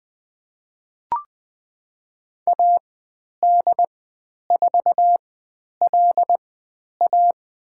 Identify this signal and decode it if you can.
Morse code, heard as a keyed tone, about 20 wpm, spaced as for 10 wpm, 700 Hz. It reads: AD4LA
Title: 095 Lesson 16g - Letter c - Call Signs 20wpm